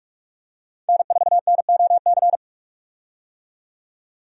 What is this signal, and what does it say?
Morse code, heard as a keyed tone, about 46 wpm, 700 Hz. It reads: N4NOC
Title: USA Call Signs - 5 Characters - 45wpm